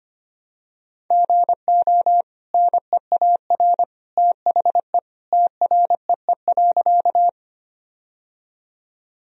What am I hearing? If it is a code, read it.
Morse code, heard as a keyed tone, about 25 wpm, 700 Hz. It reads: GO NEAR THE TREE.
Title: Sentences from Top 1000 Words - Review Part 1 - 25wpm